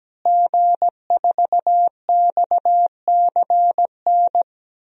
Morse code, heard as a keyed tone, about 17 wpm, 700 Hz. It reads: G4XCN
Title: DXCC-100 Call Signs - 5 Characters - Repeated 2x - 17wpm